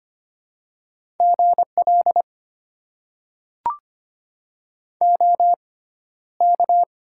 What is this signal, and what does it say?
Morse code, heard as a keyed tone, about 25 wpm, spaced as for 13 wpm, 690 Hz. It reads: GL OK
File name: Top 90 QSO Elements - 6x Character Spacing - 25wpm